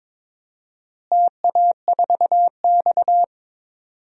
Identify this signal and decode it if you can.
Morse code, heard as a keyed tone, about 21 wpm, 700 Hz. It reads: TA4X
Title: DXCC-100 Call Signs - 4 Characters - Repeated 3x - 22wpm